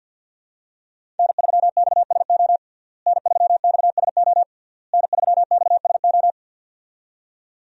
Morse code, heard as a keyed tone, about 52 wpm, 700 Hz. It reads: N3XSO N3XSO N3XSO
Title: USA Call Signs - 5 Characters - Repeated 3x - 50wpm